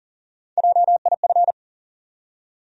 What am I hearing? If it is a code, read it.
Morse code, heard as a keyed tone, about 40 wpm, 700 Hz. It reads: JIF
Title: USA Call Sign Postfixes - 3 Letters 40wpm